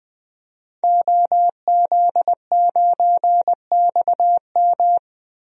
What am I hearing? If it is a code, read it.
Morse code, heard as a keyed tone, about 20 wpm, 700 Hz. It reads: OZ9XM
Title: DXCC-100 Call Signs - 5 Characters - Repeated 2x - 20wpm